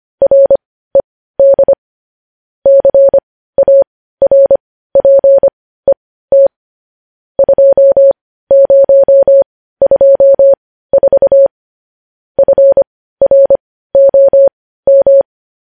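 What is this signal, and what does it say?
Morse code, heard as a keyed tone, about 25 wpm, spaced as for 15 wpm, 550 Hz. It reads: RED CARPET 2024 FROM
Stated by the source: Morse practice of news headlines at 15 wpm.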